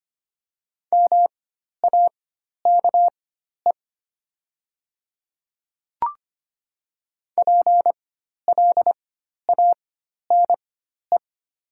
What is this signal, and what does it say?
Morse code, heard as a keyed tone, about 25 wpm, spaced as for 12 wpm, 700 Hz. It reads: MAKE PLANE
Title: 4x Character Spacing - Top 500 Words - 25wpm